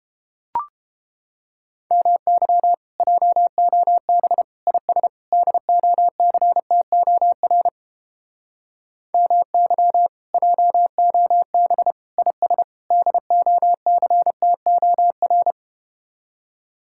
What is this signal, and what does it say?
Morse code, heard as a keyed tone, about 32 wpm, 700 Hz. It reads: MY JOB IS DOCTOR MY JOB IS DOCTOR